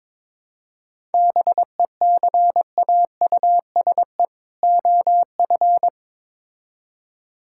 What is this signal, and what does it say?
Morse code, heard as a keyed tone, about 22 wpm, 700 Hz. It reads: BECAUSE OF
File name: Build A Phrase - 2-Word Phrases - 22wpm